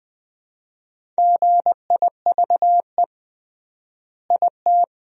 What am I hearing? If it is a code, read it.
Morse code, heard as a keyed tone, about 20 wpm, 700 Hz. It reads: GIVE IT
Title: 3x Word-Spacing - Sentences from Top 200 Words - 20wpm